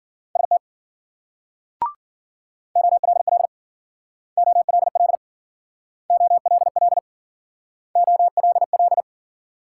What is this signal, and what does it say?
Morse code, heard as a keyed tone, about 54 wpm, 700 Hz. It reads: ST OLL OLL OLL OLL